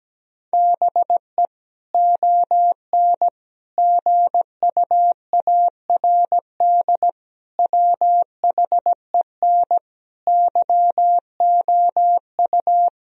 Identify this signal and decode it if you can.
Morse code, heard as a keyed tone, about 17 wpm, 700 Hz. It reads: BE ON GUARD WHEN YOU
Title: Sentences from Top 1000 Words - Review Part 4 - 17wpm